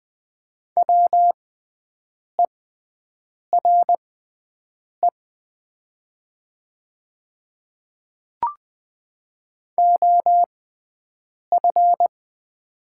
Morse code, heard as a keyed tone, about 20 wpm, spaced as for 7 wpm, 700 Hz. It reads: WERE OF
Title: 6x Character Spacing - Top 100 Words - 20wpm